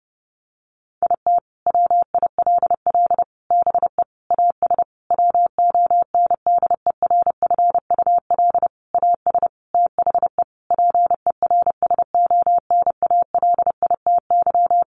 Morse code, heard as a keyed tone, about 30 wpm, 700 Hz. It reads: IT WILL BE AS WONDERFUL AS THE PERSONALITY
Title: Sentences from Top 1400 Words 30wpm